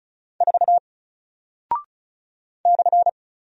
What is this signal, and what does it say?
Morse code, heard as a keyed tone, about 34 wpm, 700 Hz. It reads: 4 /
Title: Single Character (Letter:Number:Punctuation:Prosign) Alt 35wpm